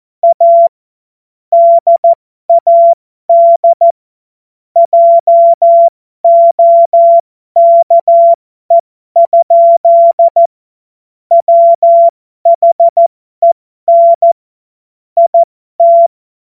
Morse code, heard as a keyed tone, about 14 wpm, spaced as for 12 wpm, 680 Hz. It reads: A DAD JOKE? WHEN IT
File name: Morsekurs / Witze